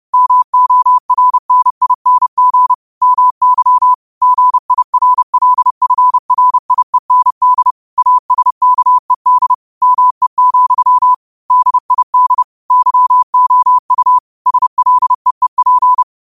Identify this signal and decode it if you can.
Morse code, heard as a keyed tone, about 30 wpm, 1000 Hz. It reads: MORNING MY GIRLFRIEND ASKED ME, DID YOU SLEEP
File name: qotdfiles / 30wpm